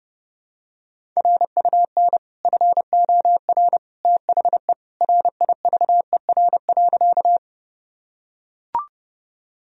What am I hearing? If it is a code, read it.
Morse code, heard as a keyed tone, about 30 wpm, 700 Hz. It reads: RUN FOR THE RIVER.
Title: Sentences from Top 1000 Words - Review Part 1 - 30wpm